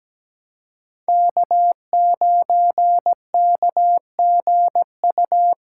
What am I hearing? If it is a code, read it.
Morse code, heard as a keyed tone, about 17 wpm, 700 Hz. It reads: K9KGU
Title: USA Callsign DE Callsign - 5 Characters - 17wpm